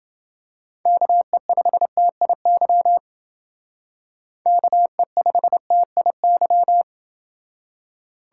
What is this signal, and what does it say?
Morse code, heard as a keyed tone, about 30 wpm, 700 Hz. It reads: KE5TIY KE5TIY